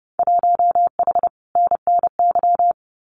Morse code, heard as a keyed tone, about 30 wpm, 700 Hz. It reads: 1H NNY
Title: Winter Field Day Exchanges 30wpm